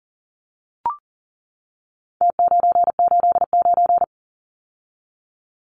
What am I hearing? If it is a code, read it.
Morse code, heard as a keyed tone, about 40 wpm, 700 Hz. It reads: T989
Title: Numbers - Contest Style - 4-Digit 40wpm